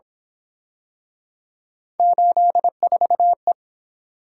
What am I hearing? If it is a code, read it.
Morse code, heard as a keyed tone, about 26 wpm, 700 Hz. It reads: 84E
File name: Speed-Racing - Three Letter-Number Combinations - 22wpm